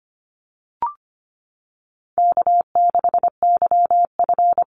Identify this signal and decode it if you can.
Morse code, heard as a keyed tone, about 25 wpm, 700 Hz. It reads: K6YF
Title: Easy Call Signs - 25wpm